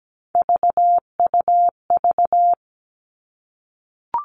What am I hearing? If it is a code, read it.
Morse code, heard as a keyed tone, about 17 wpm, 700 Hz. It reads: VUV
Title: Three Challenging Letter-Number Combinations 17wpm